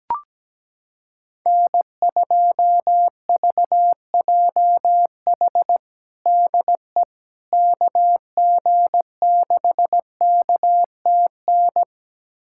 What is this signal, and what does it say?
Morse code, heard as a keyed tone, about 17 wpm, 700 Hz. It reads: N2VJH DE KG6KTN
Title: USA Callsign DE Callsign 17wpm